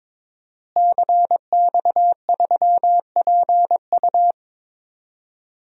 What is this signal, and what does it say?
Morse code, heard as a keyed tone, about 22 wpm, 700 Hz. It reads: CX3PU
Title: DXCC-100 Call Signs - 5 Characters - Repeated 2x - 22wpm